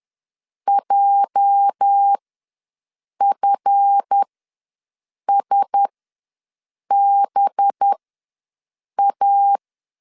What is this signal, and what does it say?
Morse code, heard as a keyed tone, about 10 wpm, spaced as for 6 wpm, 800 Hz. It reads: JFSBA
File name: LEZIONI / 10 SIGNS / 10B INT ws 10wpm 4134